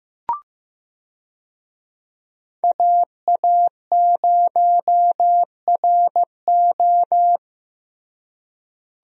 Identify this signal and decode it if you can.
Morse code, heard as a keyed tone, about 15 wpm, 700 Hz. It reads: AA0RO
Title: USA Call Signs - 5 Characters - Repeated 3x - 15wpm